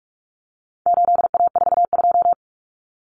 Morse code, heard as a keyed tone, about 44 wpm, 700 Hz. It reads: 8A42
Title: Common Cut Numbers - 4-Digit 45wpm